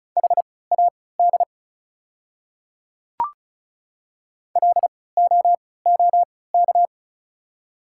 Morse code, heard as a keyed tone, about 36 wpm, spaced as for 20 wpm, 700 Hz. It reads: HAD LOOK
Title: 3x Character Spacing - Top 200 Words - 35wpm